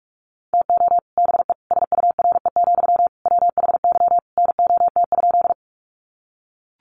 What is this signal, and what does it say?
Morse code, heard as a keyed tone, about 44 wpm, 700 Hz. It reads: TO BE SURE, WHY NOT?
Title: Sentences from Top 600 Words 45wpm